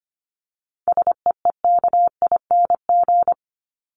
Morse code, heard as a keyed tone, about 25 wpm, 700 Hz. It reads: SEEKING
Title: ING Suffix - Top 100 Words with Base Words 25wpm